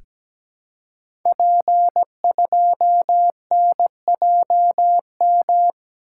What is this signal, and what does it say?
Morse code, heard as a keyed tone, about 17 wpm, 700 Hz. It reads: P2NJM
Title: Code Groups - 5 Letter:Numbers - 17wpm